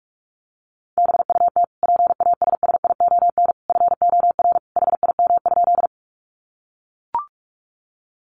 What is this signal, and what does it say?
Morse code, heard as a keyed tone, about 44 wpm, 700 Hz. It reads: BUT PASSION FOR HIM?
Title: Sentences from Top 1300 Words 45wpm